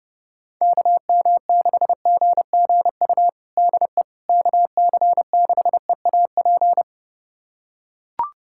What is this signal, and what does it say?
Morse code, heard as a keyed tone, about 32 wpm, 700 Hz. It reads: KM6GGU DE KC6EAP
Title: USA Callsign DE Callsign 30wpm